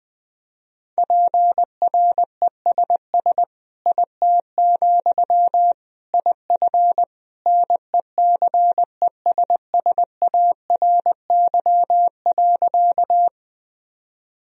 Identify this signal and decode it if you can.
Morse code, heard as a keyed tone, about 20 wpm, 700 Hz. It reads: PRESS IT, IF NECESSARY.
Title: Sentences from Top 1000 Words - Review Part 1 - 20wpm